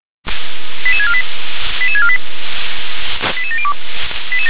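Five-tone selective call (ZVEI), heard as quick runs of five tones, tones of about 70 ms. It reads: 88648 88648 88702
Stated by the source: ZVEI five-tone selective call 88646, received by SDR.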